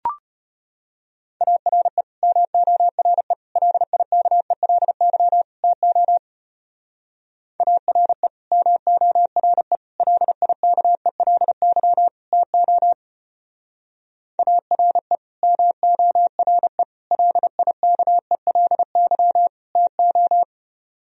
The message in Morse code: ARE MORE LIKELY TO ARE MORE LIKELY TO ARE MORE LIKELY TO

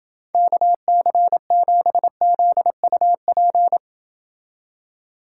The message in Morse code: KC7ZUP

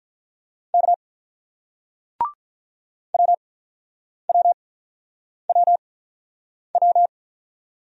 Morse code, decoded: K WWWW

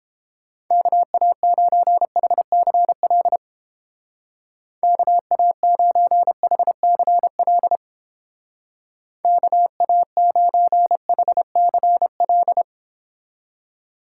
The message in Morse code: KA9HCL KA9HCL KA9HCL

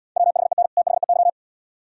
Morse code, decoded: WHEN THEY

